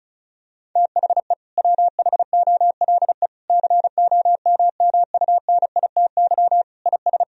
Morse code, heard as a keyed tone, about 34 wpm, 700 Hz. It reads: THE WHOLE COMMUNITY IS
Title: Sentences from Top 1000 Words - Review Part 4 - 35wpm